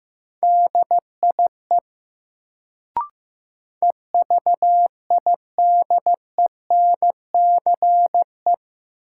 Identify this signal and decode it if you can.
Morse code, heard as a keyed tone, about 15 wpm, 700 Hz. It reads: DIE EVIDENCE
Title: Sentences from Top 1300 Words - New Word Drill - 15wpm